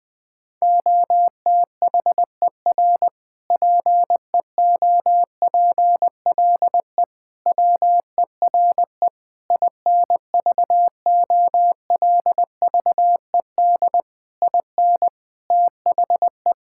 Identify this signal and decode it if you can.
Morse code, heard as a keyed tone, about 20 wpm, 700 Hz. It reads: OTHER PEOPLE WERE INVOLVED IN THE
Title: Sentences from Top 1000 Words - Review Part 5 - 20wpm